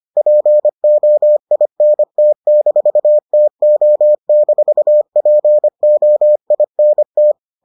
Morse code, heard as a keyed tone, about 25 wpm, 600 Hz. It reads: POINT-TO-POINT